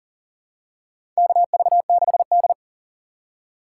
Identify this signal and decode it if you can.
Morse code, heard as a keyed tone, about 40 wpm, 700 Hz. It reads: KV6D
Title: USA Call Signs - 4 Characters - 6x Character Spacing - 40wpm